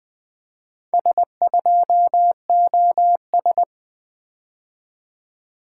Morse code, heard as a keyed tone, about 24 wpm, 700 Hz. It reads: S2OS
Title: Speed-Racing - Four Letter-Number Combinations - 20wpm